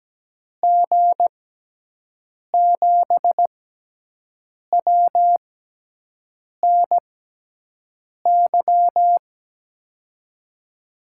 Morse code, heard as a keyed tone, about 17 wpm, spaced as for 6 wpm, 700 Hz. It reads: G7WNY
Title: USA Call Signs - 6x Character Spacing - 17wpm